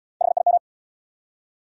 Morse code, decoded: HIM

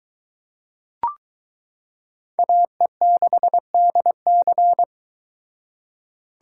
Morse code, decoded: AE6DC

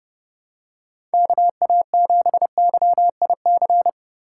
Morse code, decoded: KA7YIC

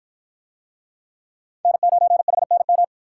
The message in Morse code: N9HNM

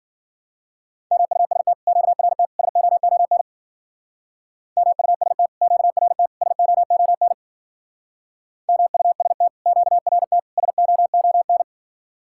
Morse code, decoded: MUST QRT SOON MUST QRT SOON MUST QRT SOON

